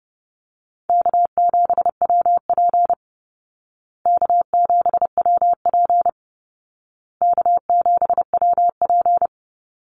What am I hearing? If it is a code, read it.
Morse code, heard as a keyed tone, about 30 wpm, 700 Hz. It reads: K7WP K7WP K7WP
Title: Easy Call Signs Repeated 3x - 30wpm